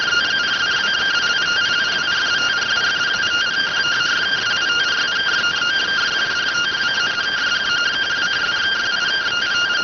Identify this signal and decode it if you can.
Two-tone teleprinter signal, about 170 Hz shift.